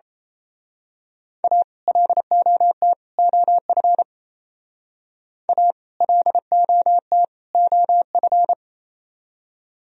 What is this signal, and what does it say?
Morse code, heard as a keyed tone, about 28 wpm, 700 Hz. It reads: A LOT OF A LOT OF
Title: Speed-Racing - Sets of 3 Words 28wpm